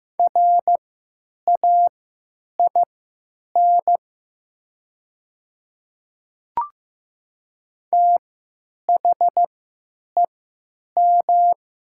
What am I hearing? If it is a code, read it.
Morse code, heard as a keyed tone, about 15 wpm, spaced as for 9 wpm, 700 Hz. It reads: RAIN THEM
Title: 3x Character Spacing - Top 500 Words - 15wpm